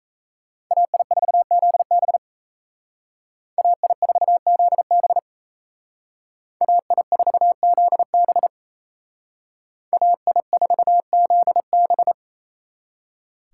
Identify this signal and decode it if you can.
Morse code, heard as a keyed tone, about 33 wpm, 700 Hz. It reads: AI4ZB AI4ZB AI4ZB AI4ZB
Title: Speed Racing - USA 2x2 Call Signs 28wpm